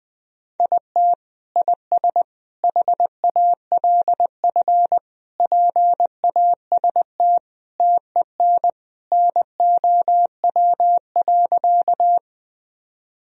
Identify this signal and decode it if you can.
Morse code, heard as a keyed tone, about 21 wpm, 700 Hz. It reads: IT IS HALF PAST TEN NOW.
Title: Sentences from Top 1000 Words - Review Part 3 - 20wpm